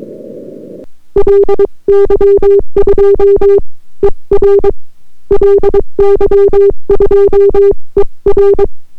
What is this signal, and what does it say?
Morse code, heard as a keyed tone, about 22 wpm, spaced as for 19 wpm, 400 Hz. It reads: LY2ER LY2ER